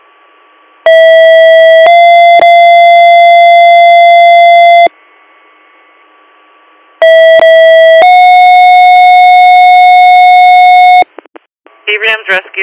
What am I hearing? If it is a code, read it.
Two-tone page, heard as a one-second tone, then a three-second tone, about 650 then 690 Hz, about 650 then 730 Hz, with speech. It is heard from 0.9 to 11.0 s.